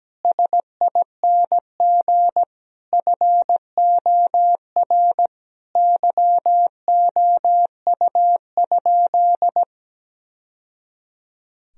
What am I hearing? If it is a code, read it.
Morse code, heard as a keyed tone, about 17 wpm, 700 Hz. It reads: SING FOR YOU?
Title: Sentences from Top 1000 Words - Review Part 2 - 17wpm